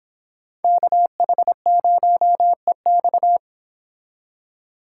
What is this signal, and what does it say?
Morse code, heard as a keyed tone, about 31 wpm, 700 Hz. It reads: KH0EX